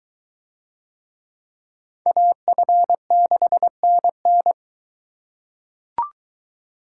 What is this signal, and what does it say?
Morse code, heard as a keyed tone, about 21 wpm, 700 Hz. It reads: AF6NN